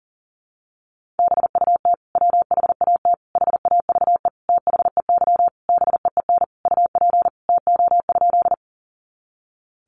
Morse code, heard as a keyed tone, about 37 wpm, 700 Hz. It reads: BUT WHAT HAVE THEY BEEN UP TO?
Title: Sentences from Top 100 Words 40wpm